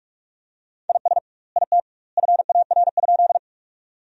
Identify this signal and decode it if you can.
Morse code, heard as a keyed tone, about 44 wpm, 700 Hz. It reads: IS IT FAR?
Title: 2x Word-Spacing - Sentences from Top 300 Words - 45wpm